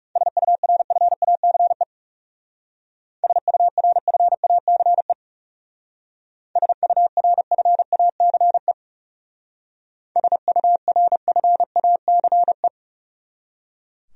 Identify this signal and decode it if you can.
Morse code, heard as a keyed tone, about 37 wpm, 700 Hz. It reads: SURFACE SURFACE SURFACE SURFACE
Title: Speed-Racing Top 500 Words 30wpm